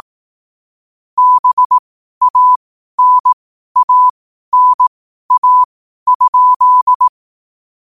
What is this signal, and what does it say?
Morse code, heard as a keyed tone, about 18 wpm, spaced as for 12 wpm, 1000 Hz. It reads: BANANA?
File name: qotdfiles / 12wpm